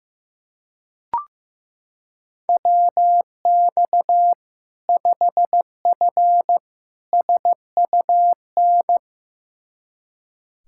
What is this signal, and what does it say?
Morse code, heard as a keyed tone, about 15 wpm, 700 Hz. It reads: WX 5F SUN
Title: Practice QSO - Exchange 2 - Info - 15wpm